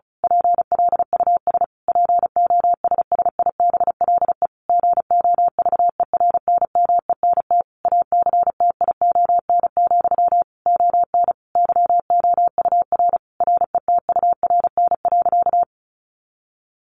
Morse code, heard as a keyed tone, about 34 wpm, 700 Hz. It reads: PLUS POSSIBLE GOVERNMENT ACTION, ON YOUR RETURN.
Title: A Sound Of Thunder 35wpm